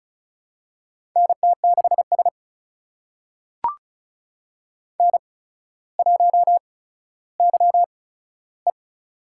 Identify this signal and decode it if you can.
Morse code, heard as a keyed tone, about 36 wpm, spaced as for 15 wpm, 700 Hz. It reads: NT6S N1YE
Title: USA Call Signs - 4 Characters - 8x Character Spacing - 35wpm